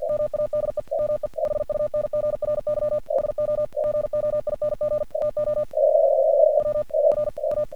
Morse code, heard as a keyed tone, about 47 wpm, 1200 Hz, with noise. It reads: MADE ME HUNGRY SO GOING TO WRA
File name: archived / CRSnet